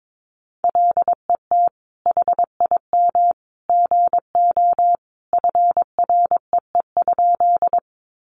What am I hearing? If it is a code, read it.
Morse code, heard as a keyed tone, about 22 wpm, 700 Hz. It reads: LET HIM GO FREE?
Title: Sentences from Top 500 Words 22wpm